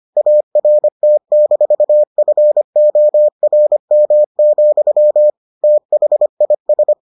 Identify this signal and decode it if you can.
Morse code, heard as a keyed tone, about 25 wpm, 600 Hz. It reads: ART-FORM, THIS